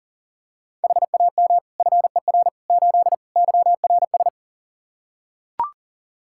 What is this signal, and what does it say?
Morse code, heard as a keyed tone, about 40 wpm, 700 Hz. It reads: HAM FER 8 YRS
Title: Practice QSO - Exchange 3 - More Info - 40wpm